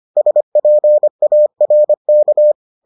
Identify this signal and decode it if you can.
Morse code, heard as a keyed tone, about 25 wpm, 600 Hz. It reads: SPARK